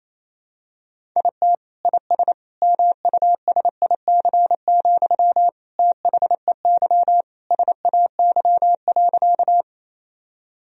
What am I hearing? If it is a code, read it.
Morse code, heard as a keyed tone, about 28 wpm, 700 Hz. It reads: IT IS MUSIC, THEY SAY.
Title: Sentences from Top 1000 Words - Review Part 1 - 28wpm